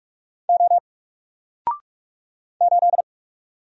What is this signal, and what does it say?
Morse code, heard as a keyed tone, about 44 wpm, 700 Hz. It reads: O 8
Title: Single Character (Letter:Number:Punctuation:Prosign) 45wpm